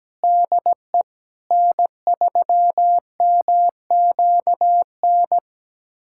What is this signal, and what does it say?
Morse code, heard as a keyed tone, about 17 wpm, 700 Hz. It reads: DE N3MQN
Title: USA Callsign DE Callsign 17wpm